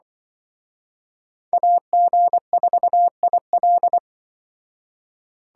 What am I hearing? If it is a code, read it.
Morse code, heard as a keyed tone, about 24 wpm, 700 Hz. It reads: AG4IL
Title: Speed Racing - USA 2x2 Call Signs 20wpm